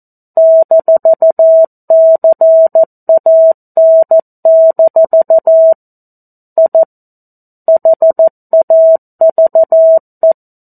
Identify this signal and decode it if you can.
Morse code, heard as a keyed tone, about 14 wpm, 650 Hz. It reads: -CAN- I HAVE
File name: 02-Chapter-02 14 WPM